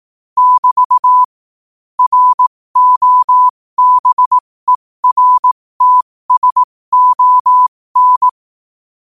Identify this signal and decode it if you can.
Morse code, heard as a keyed tone, about 18 wpm, spaced as for 15 wpm, 1000 Hz. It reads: <BT> ROBERTSON
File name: qotdfiles / 15wpm